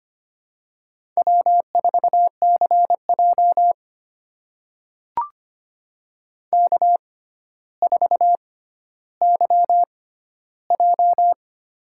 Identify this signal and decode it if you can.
Morse code, heard as a keyed tone, about 25 wpm, spaced as for 13 wpm, 700 Hz. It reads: W4CJ K4YJ
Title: USA Call Signs - 4 Characters - 6x Character Spacing - 25wpm